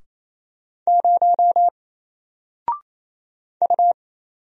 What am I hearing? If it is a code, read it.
Morse code, heard as a keyed tone, about 27 wpm, 700 Hz. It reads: 0 U
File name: Single-Letter-Number-28wpm